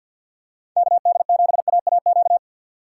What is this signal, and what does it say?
Morse code, heard as a keyed tone, about 50 wpm, 700 Hz. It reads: KD7AAQ
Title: USA Call Signs - 6 Characters - 3x Character Spacing - 50wpm